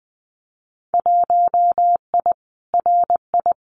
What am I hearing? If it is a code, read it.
Morse code, heard as a keyed tone, about 19 wpm, 700 Hz. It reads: 1I RI
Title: Winter Field Day Exchanges Encore 20wpm